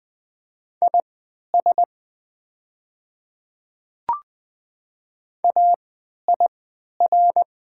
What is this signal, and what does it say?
Morse code, heard as a keyed tone, about 20 wpm, spaced as for 11 wpm, 700 Hz. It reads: IS AIR
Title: 3x Character Spacing - Top 200 Words - 20wpm